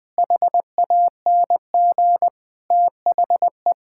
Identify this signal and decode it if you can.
Morse code, heard as a keyed tone, about 20 wpm, 700 Hz. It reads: HANG THE
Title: Sentences from Top 1600 Words 20wpm